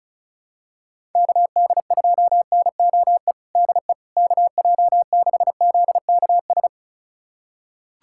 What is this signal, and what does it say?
Morse code, heard as a keyed tone, about 36 wpm, 700 Hz. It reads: KD2NOE DE KJ6ZKS
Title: USA Callsign DE Callsign - 6 Characters - 35wpm